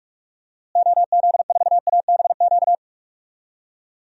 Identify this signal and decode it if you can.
Morse code, heard as a keyed tone, about 45 wpm, 700 Hz. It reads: OZ4ABQ
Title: DXCC-100 Call Signs - 6 Characters - Repeated 2x - 45wpm